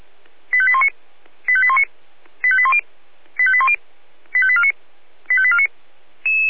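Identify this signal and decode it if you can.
Five-tone selective call (ZVEI), heard as quick runs of five tones, tones of about 70 ms. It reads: 85819 85819 85810 85810 85849 85849
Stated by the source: ZVEI five-tone selective call 85870, received by SDR.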